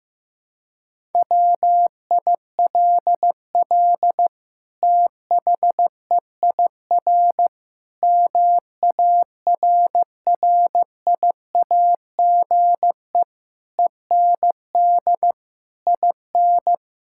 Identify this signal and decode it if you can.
Morse code, heard as a keyed tone, about 15 wpm, 700 Hz. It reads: WILL THEIR MARRIAGE END IN
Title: Sentences from Top 1000 Words - Review Part 6 - 15wpm